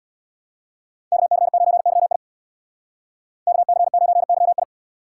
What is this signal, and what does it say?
Morse code, heard as a keyed tone, about 66 wpm, 700 Hz. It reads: KC9YI KC9YI